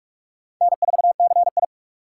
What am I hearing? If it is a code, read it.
Morse code, heard as a keyed tone, about 55 wpm, 700 Hz. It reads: N4CI
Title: Speed Racing - USA 1x2 Call Signs 45wpm